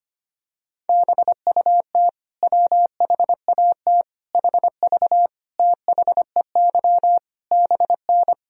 Morse code, heard as a keyed tone, about 25 wpm, 700 Hz. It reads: BUT WHAT HV THEY BN
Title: Sentences from Top 100 Words and Abbreviations 25wpm